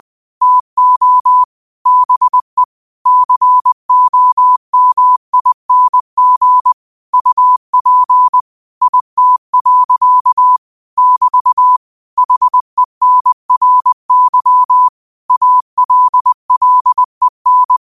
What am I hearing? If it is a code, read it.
Morse code, heard as a keyed tone, about 20 wpm, 1000 Hz. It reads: TO BE COMING UP IT. <BT> HENRY ALLEN